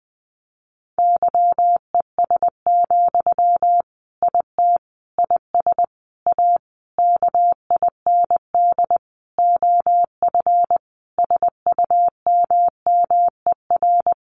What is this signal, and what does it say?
Morse code, heard as a keyed tone, about 20 wpm, 700 Hz. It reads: YES, IT IS A KIND OF SUMMER
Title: Sentences from Top 1400 Words 20wpm